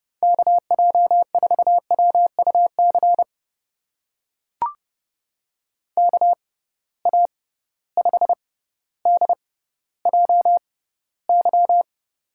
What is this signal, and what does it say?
Morse code, heard as a keyed tone, about 30 wpm, spaced as for 15 wpm, 700 Hz. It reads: KJ4WUC KA5DJY
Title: USA Call Signs - 6 Characters - 6x Character Spacing - 30wpm